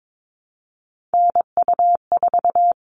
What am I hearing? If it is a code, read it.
Morse code, heard as a keyed tone, about 22 wpm, 700 Hz. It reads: NU4
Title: All Cut Numbers - 3-Digit 22wpm